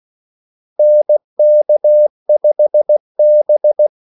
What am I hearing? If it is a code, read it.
Morse code, heard as a keyed tone, about 16 wpm, 600 Hz. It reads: NK5B
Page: Kaldesignaler 16 wpm